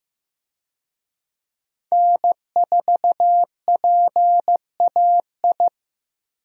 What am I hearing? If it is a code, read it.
Morse code, heard as a keyed tone, about 15 wpm, 700 Hz. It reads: N4PAI
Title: USA Call Signs - 5 Characters - Repeated 2x - 15wpm